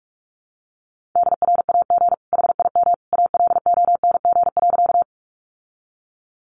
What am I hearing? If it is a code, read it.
Morse code, heard as a keyed tone, about 44 wpm, 700 Hz. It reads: DRAG HIM ALONG.
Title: Sentences from Top 1500 Words 45wpm